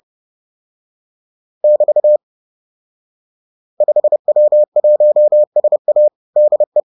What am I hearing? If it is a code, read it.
Morse code, heard as a keyed tone, about 30 wpm, 600 Hz. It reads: <BT> 5W1SA DE